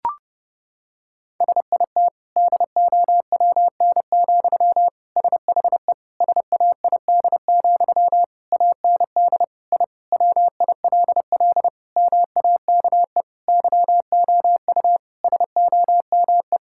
Morse code, heard as a keyed tone, about 30 wpm, 700 Hz. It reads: SIT DOWN, SHE SAID, AND I WILL MAKE YOU SOME
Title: Sentences from Top 1000 Words - Review Part 6 - 30wpm